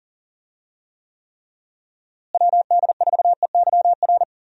Morse code, heard as a keyed tone, about 40 wpm, 700 Hz. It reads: WD4EYR
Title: USA Call Signs - 6 Characters - Repeated 3x - 40wpm